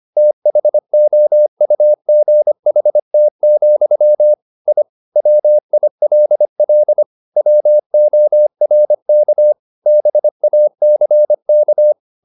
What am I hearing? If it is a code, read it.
Morse code, heard as a keyed tone, about 25 wpm, 600 Hz. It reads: THOUGHT, I WILL WORK BACK
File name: the old man and the sea part 1 25wpm